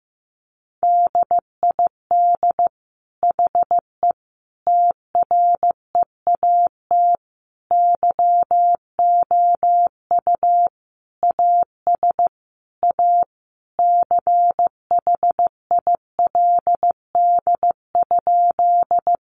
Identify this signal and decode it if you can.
Morse code, heard as a keyed tone, about 15 wpm, 700 Hz. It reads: DID HE TREAT YOU AS A CHILD?
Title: Sentences from Top 1300 Words 15wpm